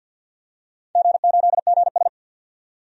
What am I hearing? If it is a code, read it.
Morse code, heard as a keyed tone, about 51 wpm, 700 Hz. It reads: G8GS
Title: DXCC-100 Call Signs - 4 Characters - 50wpm